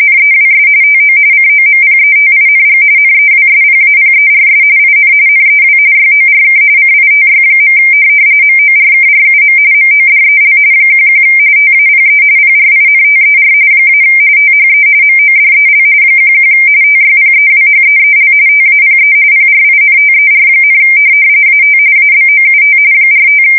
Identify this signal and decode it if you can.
RTTY, about 45.45 baud. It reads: NEWS TO COMPETING NEWSPAPERS IN NEW YORK CITY. MORKRUM MODEL 11 TAPE PRINTER - THE MODEL 11 TYPEWHEEL TAPE PRINTER, AT ABOUT 45
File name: fsk170 / teletype-corp